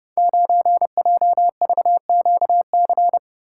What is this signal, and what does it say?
Morse code, heard as a keyed tone, about 30 wpm, 700 Hz. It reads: 9JVQC
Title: Code Groups - 5 Letter:Numbers - 30wpm